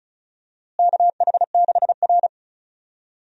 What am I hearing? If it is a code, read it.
Morse code, heard as a keyed tone, about 36 wpm, 700 Hz. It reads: KH6R
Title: USA Call Signs - 4 Characters - Repeated 3x - 35wpm